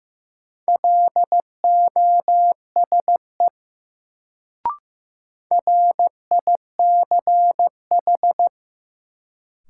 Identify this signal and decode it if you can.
Morse code, heard as a keyed tone, about 15 wpm, 700 Hz. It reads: LOSE RICH
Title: Sentences from Top 1100 Words - New Word Drill - 15wpm